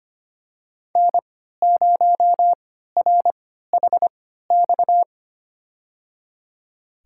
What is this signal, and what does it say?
Morse code, heard as a keyed tone, about 25 wpm, spaced as for 14 wpm, 700 Hz. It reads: N0RHX
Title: USA Call Signs - 5 Characters - 3x Character Spacing - 25wpm